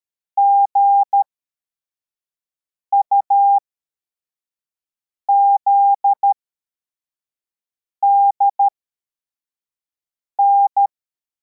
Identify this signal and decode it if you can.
Morse code, heard as a keyed tone, about 13 wpm, spaced as for 4 wpm, 800 Hz. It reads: GUZDN
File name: formation / cw / Text52